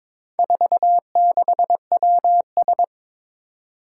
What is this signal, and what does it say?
Morse code, heard as a keyed tone, about 21 wpm, 700 Hz. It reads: 46WS
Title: Four Challenging Letter-Number Combinations 22wpm